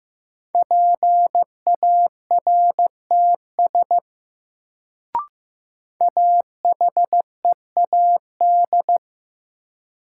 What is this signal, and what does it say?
Morse code, heard as a keyed tone, about 15 wpm, 700 Hz. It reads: PARTS AHEAD
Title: Sentences from Top 1000 Words - Review Part 6 - New Word Drill - 15wpm